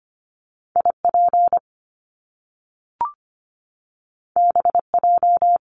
Morse code, heard as a keyed tone, about 25 wpm, 700 Hz. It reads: IP BJ
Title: Top Two Letter Combinations in Words 25wpm